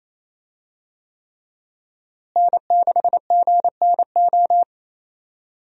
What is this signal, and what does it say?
Morse code, heard as a keyed tone, about 28 wpm, 700 Hz. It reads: N6GNO
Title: USA Call Signs - 5 Characters - Repeated 3x - 28wpm